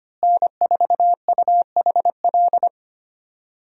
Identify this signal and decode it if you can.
Morse code, heard as a keyed tone, about 25 wpm, 700 Hz. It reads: N4UHL
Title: USA Call Signs - 5 Characters - Repeated 3x - 25wpm